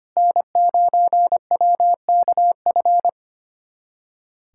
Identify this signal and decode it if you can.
Morse code, heard as a keyed tone, about 25 wpm, 700 Hz. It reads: N9WKF
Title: USA Callsign DE Callsign - 5 Characters - 25wpm